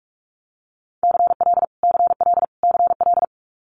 Morse code, heard as a keyed tone, about 46 wpm, 700 Hz. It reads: CL CL CL
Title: Multi-letter Phonemes Repeated 3x 45wpm